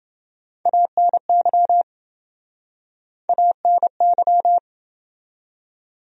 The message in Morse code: ANY ANY

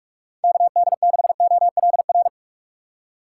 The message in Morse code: KD6OLR

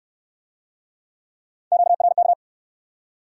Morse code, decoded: QRK

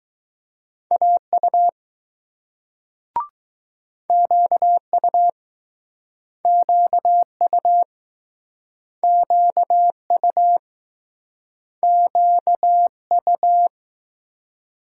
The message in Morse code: AU QU QU QU QU

